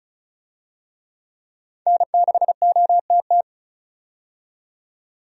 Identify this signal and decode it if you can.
Morse code, heard as a keyed tone, about 36 wpm, 700 Hz. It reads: N6OTT
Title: USA Call Signs - 5 Characters - 35wpm